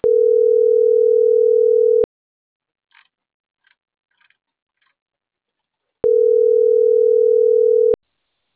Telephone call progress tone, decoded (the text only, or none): ringback tone